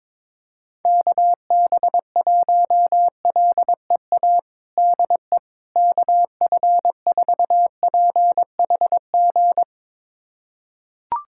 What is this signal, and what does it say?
Morse code, heard as a keyed tone, about 22 wpm, 700 Hz. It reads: KB1LEA DE KF4PHG
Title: USA Callsign DE Callsign 22wpm